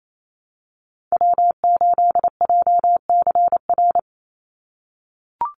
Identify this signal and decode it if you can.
Morse code, heard as a keyed tone, about 28 wpm, 700 Hz. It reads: W8JCR
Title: USA Call Signs 28wpm